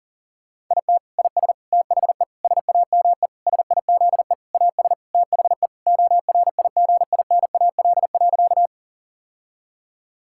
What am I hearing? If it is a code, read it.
Morse code, heard as a keyed tone, about 37 wpm, 700 Hz. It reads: IT IS THE SAME SIZE AS THE ORIGINAL.
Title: Sentences from Top 1000 Words - Review Part 6 - 40wpm